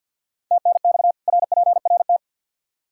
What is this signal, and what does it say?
Morse code, heard as a keyed tone, about 50 wpm, 700 Hz. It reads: TNX RPRT
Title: Practice QSO - Exchange 2 - Info - 50wpm